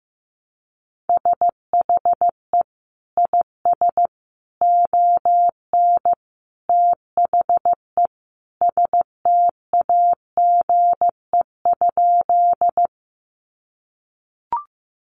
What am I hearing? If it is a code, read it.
Morse code, heard as a keyed tone, about 15 wpm, 700 Hz. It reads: SHE IS ON THE STAGE?
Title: Sentences from Top 1200 Words 15wpm